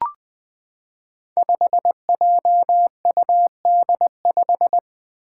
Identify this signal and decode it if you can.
Morse code, heard as a keyed tone, about 20 wpm, 700 Hz. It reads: 5JUD5
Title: Five Challenging Letter-Number Combinations 20wpm